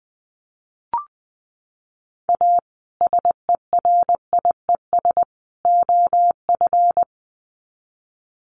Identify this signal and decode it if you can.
Morse code, heard as a keyed tone, about 20 wpm, 700 Hz. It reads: A SERIES OF
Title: Sets of 3 Words 20wpm